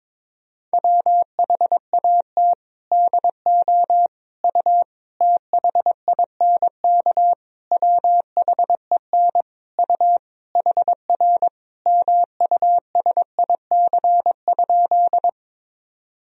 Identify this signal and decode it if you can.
Morse code, heard as a keyed tone, about 22 wpm, 700 Hz. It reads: WHAT DO U THINK WHEN U HR MUSIC?
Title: Sentences from Top 300 Words and Abbreviations 22wpm